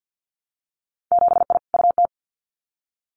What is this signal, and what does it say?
Morse code, heard as a keyed tone, about 50 wpm, 700 Hz. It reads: 7I UT
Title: Winter Field Day Exchanges Encore 50wpm